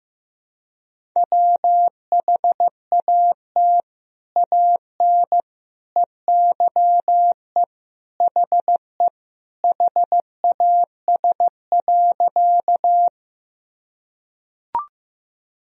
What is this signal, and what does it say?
Morse code, heard as a keyed tone, about 16 wpm, 700 Hz. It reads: WHAT AN EYE HE HAS.
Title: Sentences from Top 1000 Words - Review Part 1 - 15wpm